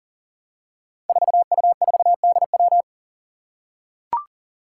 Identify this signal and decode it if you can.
Morse code, heard as a keyed tone, about 40 wpm, 700 Hz. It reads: 4U4DW
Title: Five Challenging Letter-Number Combinations 40wpm